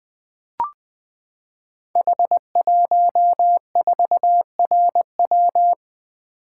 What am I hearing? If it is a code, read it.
Morse code, heard as a keyed tone, about 20 wpm, 700 Hz. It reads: H14RW
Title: Code Groups - 5 Letter:Numbers - 20wpm